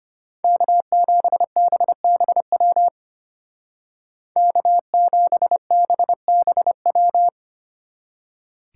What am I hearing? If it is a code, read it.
Morse code, heard as a keyed tone, about 33 wpm, 700 Hz. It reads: K7BBW K7BBW